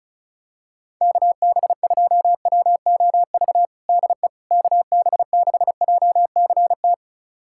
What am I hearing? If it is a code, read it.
Morse code, heard as a keyed tone, about 36 wpm, 700 Hz. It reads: KB2WOV DE KB6JCT
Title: USA Callsign DE Callsign - 6 Characters - 35wpm